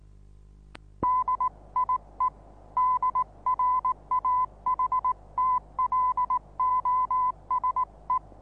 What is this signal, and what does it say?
Morse code, heard as a keyed tone, about 19 wpm, 1010 Hz. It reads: DIE DRAHTLOSE
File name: repos / Morsekurs / DJ4UF Morsen26